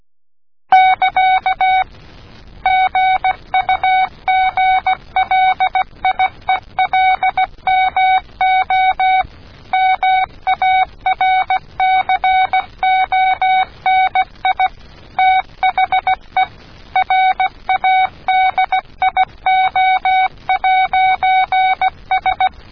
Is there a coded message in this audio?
Yes: Morse code from 0.7 to 22.5 s, heard as a keyed tone, about 16 wpm, 750 Hz. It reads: <KA> GUGLIELMO MARCONI THE RADIO'S